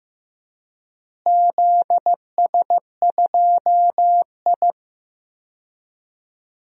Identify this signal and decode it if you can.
Morse code, heard as a keyed tone, about 15 wpm, 700 Hz. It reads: ZS2I
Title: DXCC-100 Call Signs - 4 Characters - Repeated 2x - 15wpm